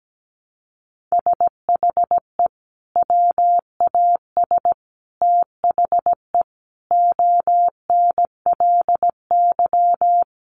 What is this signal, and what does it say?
Morse code, heard as a keyed tone, about 17 wpm, 700 Hz. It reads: SHE WAS THE ONLY
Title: Sentences from Top 1100 Words 17wpm